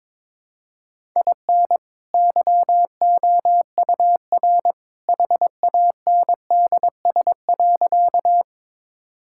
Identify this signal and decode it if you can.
Morse code, heard as a keyed tone, about 21 wpm, 700 Hz. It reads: IN YOUR HANDS.
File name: practice / 22wpm / Sentences with 2-Word Phrases - 22wpm